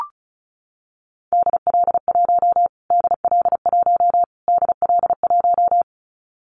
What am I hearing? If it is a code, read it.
Morse code, heard as a keyed tone, about 34 wpm, 700 Hz. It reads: DL1 DL1 DL1
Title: Callsign Prefixes for Top 25 DXCC - Repeated 3x - 35wpm